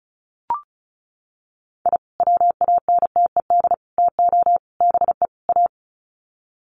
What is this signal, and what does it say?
Morse code, heard as a keyed tone, about 34 wpm, 700 Hz. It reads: I WANTED TO BE A
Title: Sets of 5 Words Encore2 35wpm